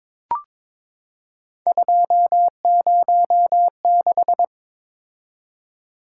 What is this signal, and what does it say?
Morse code, heard as a keyed tone, about 23 wpm, 700 Hz. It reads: 206
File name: practice / 22wpm / Three Letter-Number Combinations - 22wpm